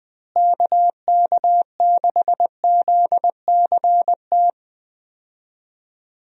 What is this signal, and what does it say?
Morse code, heard as a keyed tone, about 20 wpm, 700 Hz. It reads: KK6ZCT
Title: USA Call Signs - 6 Characters - Repeated 3x - 20wpm